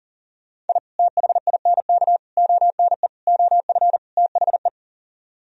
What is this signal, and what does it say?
Morse code, heard as a keyed tone, about 52 wpm, 700 Hz. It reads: I THINK ONE OF THE